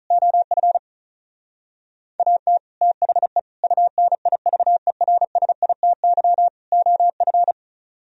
Morse code, heard as a keyed tone, about 46 wpm, 700 Hz. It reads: OF AT THE UNIVERSITY OF